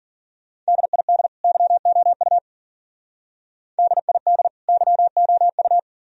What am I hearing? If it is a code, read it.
Morse code, heard as a keyed tone, about 52 wpm, 700 Hz. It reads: DID YOU DID YOU